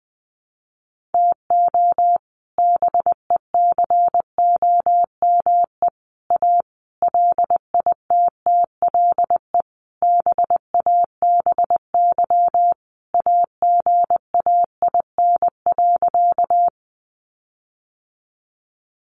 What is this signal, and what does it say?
Morse code, heard as a keyed tone, about 20 wpm, 700 Hz. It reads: TO BECOME A LITTLE BABY AGAIN.
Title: Sentences from Top 1300 Words 20wpm